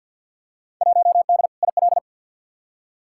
Morse code, heard as a keyed tone, about 50 wpm, 700 Hz. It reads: 1D IL
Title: Field Day Exchanges - 50wpm